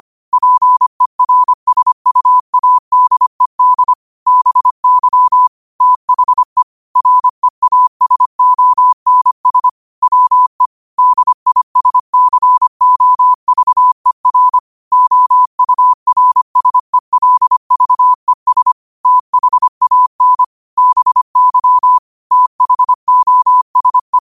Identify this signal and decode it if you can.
Morse code, heard as a keyed tone, about 25 wpm, 1000 Hz. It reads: PERSUADED BY THE REASONS WE DISCOVER OURSELVES THAN BY THOSE